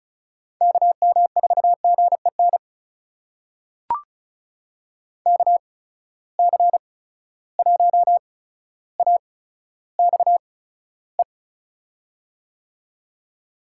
Morse code, heard as a keyed tone, about 36 wpm, spaced as for 15 wpm, 700 Hz. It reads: KM4GEN KC1AXE
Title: USA Call Signs - 6 Characters - 8x Character Spacing - 35wpm